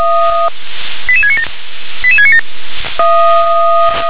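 Five-tone selective call (ZVEI), heard as quick runs of five tones, tones of about 70 ms. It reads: 88697 88697
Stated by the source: ZVEI five-tone selective call received by SDR.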